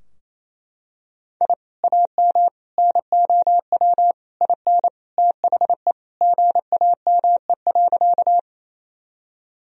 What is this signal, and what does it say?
Morse code, heard as a keyed tone, about 28 wpm, 700 Hz. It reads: I AM NOW IN THE GAME.
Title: Sentences from Top 1000 Words - Review Part 3 - 28wpm